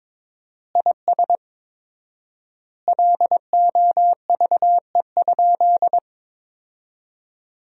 Morse code, heard as a keyed tone, about 21 wpm, 700 Hz. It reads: IS LOVE?
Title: 4x Word-Spacing - Sentences from Top 400 Words - 22wpm